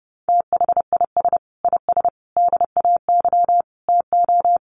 Morse code, heard as a keyed tone, about 30 wpm, 700 Hz. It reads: THIS IS DAY TO
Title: Sets of 5 Words Encore 30wpm